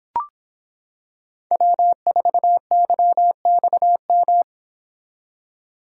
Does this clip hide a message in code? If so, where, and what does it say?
Yes: Morse code from 1.5 to 4.4 s, heard as a keyed tone, about 26 wpm, 700 Hz. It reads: W4YXM